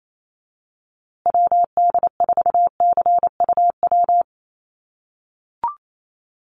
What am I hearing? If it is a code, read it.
Morse code, heard as a keyed tone, about 28 wpm, 700 Hz. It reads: WD4CUW
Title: USA Call Signs Repeated 3x 28wpm